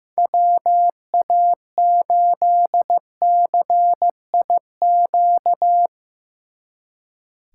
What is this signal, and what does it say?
Morse code, heard as a keyed tone, about 17 wpm, 700 Hz. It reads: WA8CIQ
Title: Speed Racing - USA Call Signs 15wpm